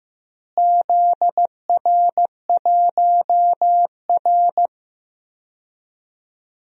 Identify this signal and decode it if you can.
Morse code, heard as a keyed tone, about 15 wpm, 700 Hz. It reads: ZR1R
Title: DXCC-100 Call Signs - 4 Characters - Repeated 3x - 15wpm